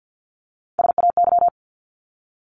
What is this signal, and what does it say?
Morse code, heard as a keyed tone, about 50 wpm, 700 Hz. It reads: SAY
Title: ING Suffix - Top 100 Words with Base Words 50wpm